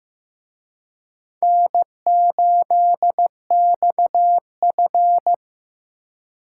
Morse code, heard as a keyed tone, about 18 wpm, 700 Hz. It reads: N8XF
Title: Speed Racing - USA Call Signs 15wpm